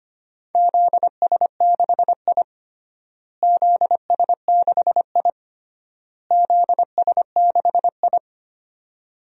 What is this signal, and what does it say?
Morse code, heard as a keyed tone, about 25 wpm, 700 Hz. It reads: ZS6I ZS6I ZS6I
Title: DXCC-100 Call Signs - 4 Characters - Repeated 3x - 25wpm